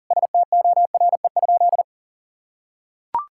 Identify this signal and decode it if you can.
Morse code, heard as a keyed tone, about 40 wpm, 700 Hz. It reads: STORE?
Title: Sentences from Top 1000 Words - Review Part 5 - 40wpm